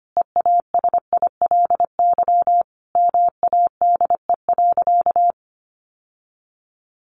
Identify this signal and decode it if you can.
Morse code, heard as a keyed tone, about 25 wpm, 700 Hz. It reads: EASILY MADE.
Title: Sentences from Top 1200 Words 25wpm